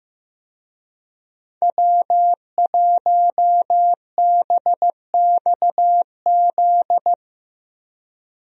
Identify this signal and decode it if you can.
Morse code, heard as a keyed tone, about 15 wpm, 700 Hz. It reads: W1BXZ
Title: USA Call Signs - 5 Characters - Repeated 2x - 15wpm